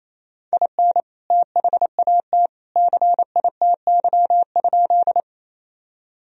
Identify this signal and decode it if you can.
Morse code, heard as a keyed tone, about 28 wpm, 700 Hz. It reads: IN THAT CITY?
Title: Sentences from Top 1000 Words - Review Part 1 - 28wpm